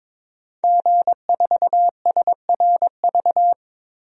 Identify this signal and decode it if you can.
Morse code, heard as a keyed tone, about 22 wpm, 700 Hz. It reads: G4SRV
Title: DXCC-100 Call Signs - 5 Characters - Repeated 3x - 22wpm